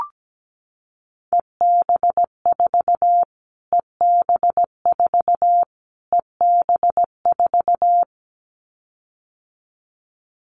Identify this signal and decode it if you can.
Morse code, heard as a keyed tone, about 17 wpm, 700 Hz. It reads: EB4 EB4 EB4
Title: Callsign Prefixes for Top 25 DXCC - Repeated 3x - 17wpm